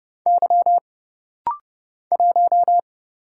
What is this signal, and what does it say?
Morse code, heard as a keyed tone, about 28 wpm, 700 Hz. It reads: Y 1
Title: Single Letter-Number - Rapid-Fire 30wpm